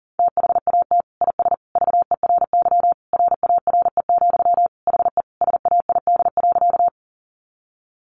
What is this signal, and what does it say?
Morse code, heard as a keyed tone, about 40 wpm, 700 Hz. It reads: THAT IS VERY RARE, HE SAID.
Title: Sentences from Top 1100 Words 40wpm